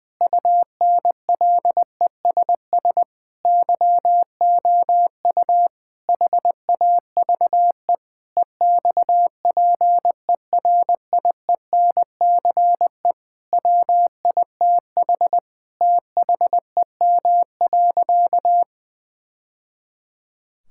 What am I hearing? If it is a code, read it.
Morse code, heard as a keyed tone, about 20 wpm, 700 Hz. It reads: UNLESS YOU HAVE EXPERIENCE WITH THEM.
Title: Sentences from Top 1000 Words - Review Part 6 - 20wpm